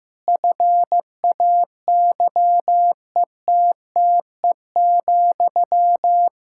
Morse code, heard as a keyed tone, about 15 wpm, 700 Hz. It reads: FAYETTE,
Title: Practice QSO - Exchange 1 - Beginning - 15wpm